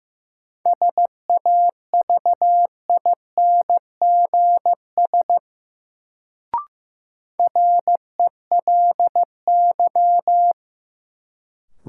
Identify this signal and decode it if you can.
Morse code, heard as a keyed tone, about 15 wpm, 700 Hz. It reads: SAVINGS RELY
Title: Sentences from Top 1200 Words - New Word Drill Alt - 15wpm